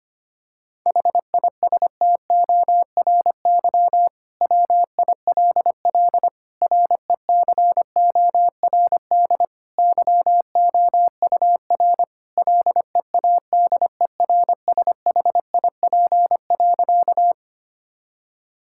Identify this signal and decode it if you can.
Morse code, heard as a keyed tone, about 25 wpm, 700 Hz. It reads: HISTORY WILL RECORD YOUR LEADERSHIP.
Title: Sentences from Top 1600 Words 25wpm